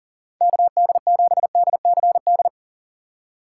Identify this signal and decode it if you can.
Morse code, heard as a keyed tone, about 40 wpm, 690 Hz. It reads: KD7DCD
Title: USA Call Signs - 4x Character Spacing - 40wpm